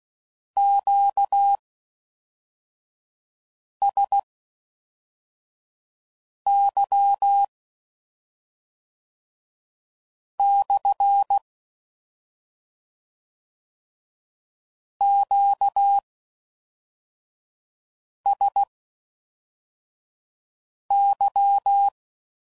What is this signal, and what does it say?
Morse code, heard as a keyed tone, about 16 wpm, spaced as for 4 wpm, 800 Hz. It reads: QSY / QSY